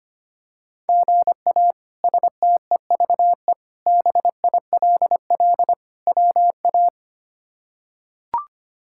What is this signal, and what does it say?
Morse code, heard as a keyed tone, about 25 wpm, 700 Hz. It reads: GA STEVE BILL WA
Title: K1USN SST QSO Practice - 25wpm